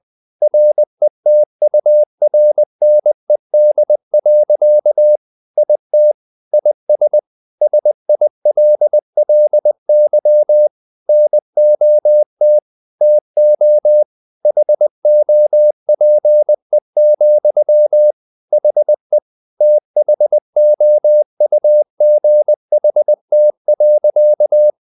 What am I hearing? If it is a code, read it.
Morse code, heard as a keyed tone, about 20 wpm, 600 Hz. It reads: RETURNED. IT IS SILLY NOT TO HOPE, HE THOUGHT.
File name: the old man and the sea part 2 20wpm